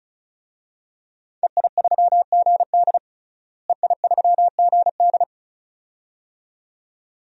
Morse code, heard as a keyed tone, about 36 wpm, 700 Hz. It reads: EI3GD EI3GD
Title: DXCC-100 Call Signs - 5 Characters - Repeated 2x - 35wpm